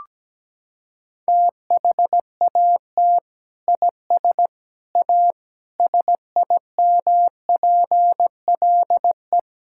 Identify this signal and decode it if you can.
Morse code, heard as a keyed tone, about 17 wpm, 700 Hz. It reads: THAT IS A SIMPLE
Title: Sentences from Top 1000 Words - Review Part 6 - 17wpm